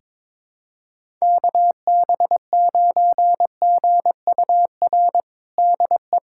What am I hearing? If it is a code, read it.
Morse code, heard as a keyed tone, about 22 wpm, 700 Hz. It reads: KB9GUR DE
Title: USA Callsign DE Callsign 22wpm